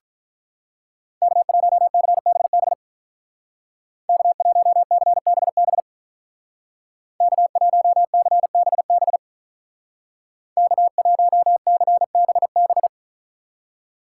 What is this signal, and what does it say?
Morse code, heard as a keyed tone, about 43 wpm, 700 Hz. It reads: K1CBB K1CBB K1CBB K1CBB
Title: Speed Racing - USA 1x3 Call Signs 35wpm